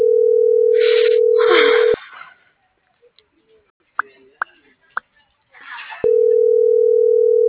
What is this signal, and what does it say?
Ringback tone (North America), 440 + 480 Hz.